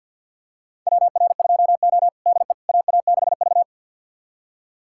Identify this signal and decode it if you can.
Morse code, heard as a keyed tone, about 52 wpm, 700 Hz. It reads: WR2O DE AA6V
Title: USA Callsign DE Callsign - 4 Characters - 50wpm